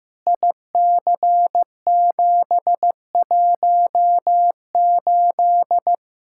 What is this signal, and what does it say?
Morse code, heard as a keyed tone, about 16 wpm, 700 Hz. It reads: IC718
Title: Practice QSO - Exchange 2 - Info - 15wpm